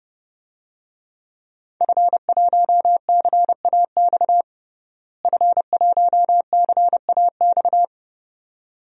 Morse code, heard as a keyed tone, about 30 wpm, 700 Hz. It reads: F1CAX F1CAX
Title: DXCC-100 Call Signs - 5 Characters - Repeated 2x - 30wpm